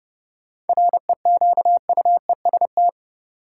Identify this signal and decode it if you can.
Morse code, heard as a keyed tone, about 30 wpm, 700 Hz. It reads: REQUEST
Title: Seven Letter Words - 30wpm